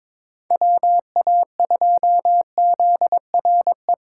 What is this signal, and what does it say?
Morse code, heard as a keyed tone, about 22 wpm, 700 Hz. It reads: WA2ZRE
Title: USA Callsign DE Callsign 22wpm